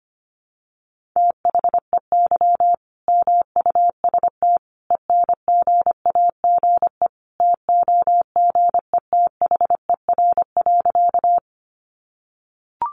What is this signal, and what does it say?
Morse code, heard as a keyed tone, about 25 wpm, 700 Hz. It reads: THEY MUST ENGAGE TOGETHER.
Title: Sentences from Top 1500 Words 25wpm